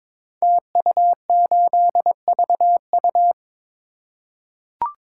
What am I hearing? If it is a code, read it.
Morse code, heard as a keyed tone, about 22 wpm, 700 Hz. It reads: TU8VU
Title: Code Groups - 5 Letter:Numbers - 22wpm